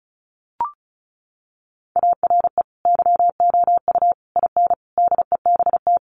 Morse code, heard as a keyed tone, about 34 wpm, 700 Hz. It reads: ARE YOU IN DEBT
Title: Sentences from Top 1300 Words 35wpm